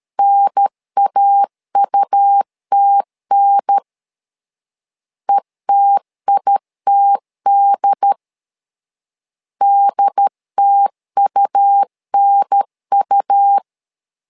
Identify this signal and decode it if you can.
Morse code, heard as a keyed tone, about 13 wpm, 800 Hz. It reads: NAUTN ETITD DTUNU
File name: LEZIONI / 2 UHDN / 2 UHDN 12wpm 1456